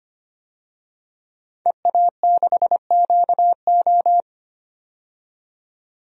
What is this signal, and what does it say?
Morse code, heard as a keyed tone, about 25 wpm, 700 Hz. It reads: EA6QO
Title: DXCC-100 Call Signs - 5 Characters - 25wpm